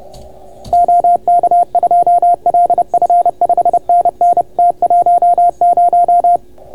Morse code, heard as a keyed tone, about 30 wpm, 690 Hz. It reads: OK2LF 5NN T10